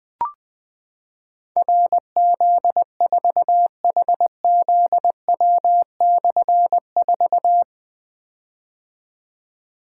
Morse code, heard as a keyed tone, about 20 wpm, 700 Hz. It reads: RZ4HZW/4
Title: Difficult Call Signs 20wpm